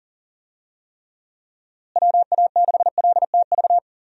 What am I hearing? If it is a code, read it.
Morse code, heard as a keyed tone, about 37 wpm, 700 Hz. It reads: WA6LTV
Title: USA Call Signs - 6 Characters - Repeated 3x - 40wpm